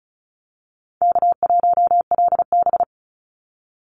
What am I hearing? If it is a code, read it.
Morse code, heard as a keyed tone, about 36 wpm, 700 Hz. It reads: K1LB
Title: NA SOTA Activator Callsigns 35wpm